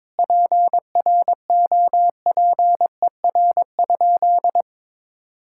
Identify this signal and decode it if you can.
Morse code, heard as a keyed tone, about 22 wpm, 700 Hz. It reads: PROPER?
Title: Sentences from Top 1000 Words - Review Part 5 - 22wpm